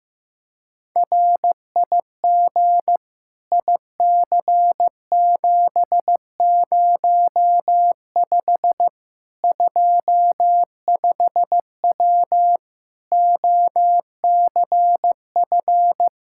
Morse code, heard as a keyed tone, about 15 wpm, 700 Hz. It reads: RIG IC705 25W OCF
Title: K1USN SST QSO Practice 15wpm